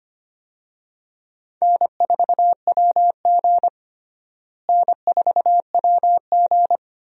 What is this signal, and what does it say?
Morse code, heard as a keyed tone, about 25 wpm, 700 Hz. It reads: N4WG N4WG
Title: USA Call Signs - 4 Characters - Repeated 2x - 25wpm